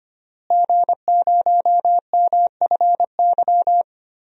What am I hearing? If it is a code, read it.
Morse code, heard as a keyed tone, about 25 wpm, 700 Hz. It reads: G0MFY
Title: DXCC-100 Call Signs - 5 Characters - Repeated 3x - 25wpm